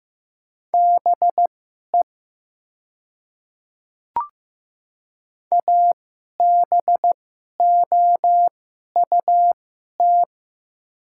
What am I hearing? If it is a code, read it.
Morse code, heard as a keyed tone, about 15 wpm, spaced as for 11 wpm, 700 Hz. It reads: BE ABOUT
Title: 2x Character Spacing - Top 300 Words - 15wpm